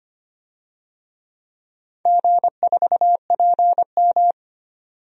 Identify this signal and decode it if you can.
Morse code, heard as a keyed tone, about 25 wpm, 700 Hz. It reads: G4PM
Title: DXCC-100 Call Signs - 4 Characters - 25wpm